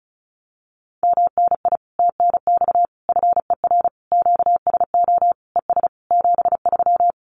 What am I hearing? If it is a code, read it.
Morse code, heard as a keyed tone, about 34 wpm, 700 Hz. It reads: MNI TNX FER QSO ES 73
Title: Common QSO Phrases Encore 35wpm